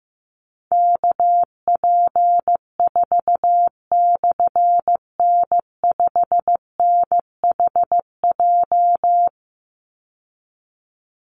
Morse code, heard as a keyed tone, about 15 wpm, 700 Hz. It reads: KP4/N5NHJ
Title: Call Signs Repeated 3x - 15wpm